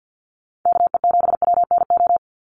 Morse code, heard as a keyed tone, about 50 wpm, 700 Hz. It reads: KE7WNO
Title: USA Call Signs 50wpm